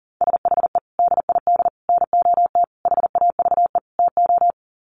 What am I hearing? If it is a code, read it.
Morse code, heard as a keyed tone, about 37 wpm, 700 Hz. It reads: SHE DID NOT HAVE TO
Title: Sets of 5 Words Encore2 40wpm